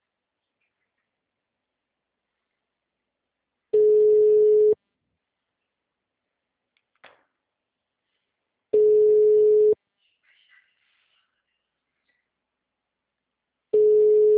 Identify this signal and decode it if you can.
Ringback tone (Europe, 425 Hz).